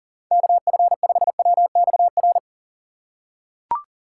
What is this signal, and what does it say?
Morse code, heard as a keyed tone, about 40 wpm, 700 Hz. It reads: KF5WXR
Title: USA Call Signs - 6 Characters - 2x Character Spacing - 40wpm